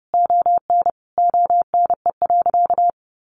Morse code, heard as a keyed tone, about 30 wpm, 700 Hz. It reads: ON ONE.
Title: Sentences from Top 1500 Words 30wpm